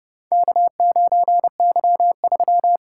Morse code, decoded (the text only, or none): K9Y3